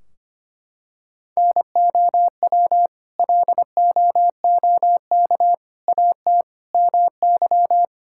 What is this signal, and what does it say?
Morse code, heard as a keyed tone, about 25 wpm, 700 Hz. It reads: NOW LOOK AT MY
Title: Sentences from Top 1600 Words 25wpm